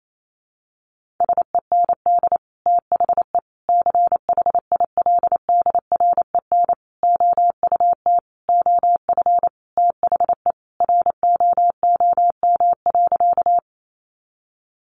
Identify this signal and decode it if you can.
Morse code, heard as a keyed tone, about 28 wpm, 700 Hz. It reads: SEND THE CHILDREN OUT OF THE ROOM.
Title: Sentences from Top 1500 Words 28wpm